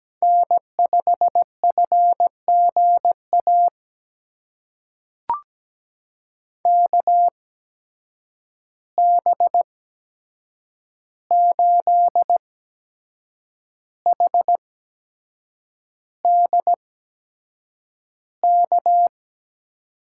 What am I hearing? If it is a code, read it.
Morse code, heard as a keyed tone, about 17 wpm, spaced as for 7 wpm, 700 Hz. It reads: N5FGA KB8HDK
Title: USA Call Signs - 8x Character Spacing - 17wpm